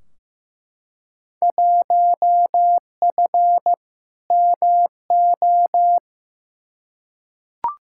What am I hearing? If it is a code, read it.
Morse code, heard as a keyed tone, about 15 wpm, 700 Hz. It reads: 1F MO
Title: Field Day Exchanges Encore - 15wpm